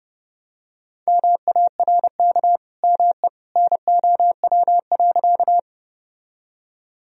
Morse code, heard as a keyed tone, about 30 wpm, 700 Hz. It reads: MARK ME NOW.
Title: Sentences from Top 1000 Words - Review Part 1 - 30wpm